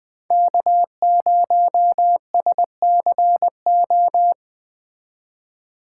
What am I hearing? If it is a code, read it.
Morse code, heard as a keyed tone, about 20 wpm, 700 Hz. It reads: K0SCO
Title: Speed Racing - USA 1x3 Call Signs 17wpm